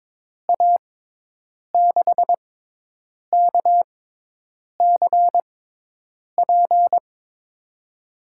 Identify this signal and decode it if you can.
Morse code, heard as a keyed tone, about 22 wpm, spaced as for 8 wpm, 700 Hz. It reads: A6KCP
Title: USA Call Signs - 6x Character Spacing - 22wpm